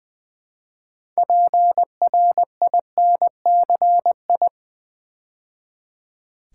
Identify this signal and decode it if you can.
Morse code, heard as a keyed tone, about 20 wpm, 700 Hz. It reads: PRINCI
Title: Build A Word - 3 Letters At A Time - 9-Letter Words - 20wpm